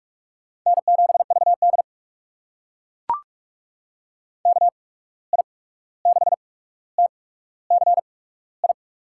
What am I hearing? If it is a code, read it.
Morse code, heard as a keyed tone, about 44 wpm, spaced as for 17 wpm, 700 Hz. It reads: N7VD KI6TCI
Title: USA Call Signs - 8x Character Spacing - 45wpm